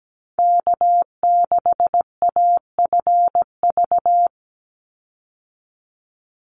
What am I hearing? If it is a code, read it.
Morse code, heard as a keyed tone, about 17 wpm, 700 Hz. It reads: K6AFV
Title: USA Call Signs 17wpm